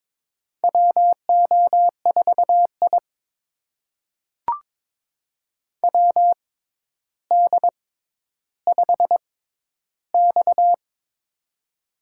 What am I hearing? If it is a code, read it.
Morse code, heard as a keyed tone, about 22 wpm, spaced as for 11 wpm, 700 Hz. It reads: WO4I WD5X
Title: USA Call Signs - 4 Characters - 6x Character Spacing - 22wpm